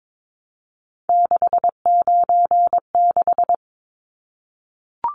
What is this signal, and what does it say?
Morse code, heard as a keyed tone, about 22 wpm, 700 Hz. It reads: 696
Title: Numbers - 3-Digit 22wpm